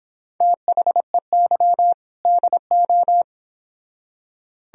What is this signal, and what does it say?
Morse code, heard as a keyed tone, about 30 wpm, 700 Hz. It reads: THEY DO